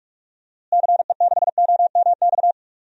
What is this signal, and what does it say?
Morse code, heard as a keyed tone, about 46 wpm, 700 Hz. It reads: CE6OMX
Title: DXCC-100 Call Signs - 6 Characters - Repeated 2x - 45wpm